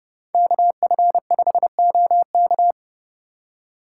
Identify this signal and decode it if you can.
Morse code, heard as a keyed tone, about 27 wpm, 700 Hz. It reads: KF5OK